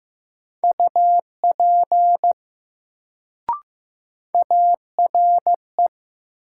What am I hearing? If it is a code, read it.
Morse code, heard as a keyed tone, about 15 wpm, 700 Hz. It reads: UP ARE
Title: Sentences from Top 100 Words - New Word Drill - 15wpm